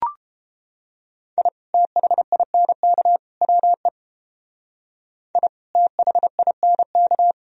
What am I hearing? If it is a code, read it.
Morse code, heard as a keyed tone, about 28 wpm, 700 Hz. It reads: I THINK WE I THINK